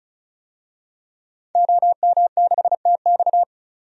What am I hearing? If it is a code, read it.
Morse code, heard as a keyed tone, about 34 wpm, 700 Hz. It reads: OM6TX
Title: DXCC-100 Call Signs - 5 Characters - 35wpm